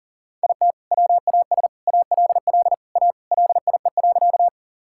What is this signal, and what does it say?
Morse code, heard as a keyed tone, about 40 wpm, 700 Hz. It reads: IT WAS ALL A LIE.
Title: Sentences from Top 1600 Words 40wpm